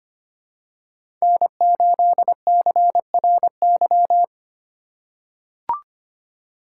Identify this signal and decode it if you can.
Morse code, heard as a keyed tone, about 25 wpm, 700 Hz. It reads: N8CRY
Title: USA Call Signs - 5 Characters - 8x Character Spacing - 25wpm